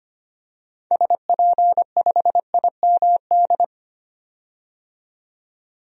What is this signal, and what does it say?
Morse code, heard as a keyed tone, about 25 wpm, 700 Hz. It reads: SP5IMD
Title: DXCC-100 Call Signs - 6 Characters - 25wpm